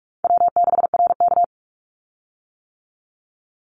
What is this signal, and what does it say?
Morse code, heard as a keyed tone, about 46 wpm, 700 Hz. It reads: W6RK
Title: Easy Call Signs - 45wpm